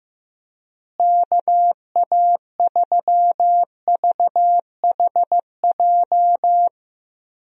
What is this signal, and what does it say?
Morse code, heard as a keyed tone, about 15 wpm, 700 Hz. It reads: KA3VHJ
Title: USA Call Signs - 6 Characters - Repeated 2x - 15wpm